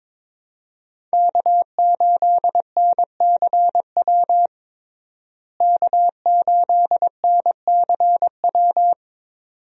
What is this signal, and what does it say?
Morse code, heard as a keyed tone, about 21 wpm, 700 Hz. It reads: K8NCW K8NCW
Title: USA Call Signs - 5 Characters - Repeated 2x - 22wpm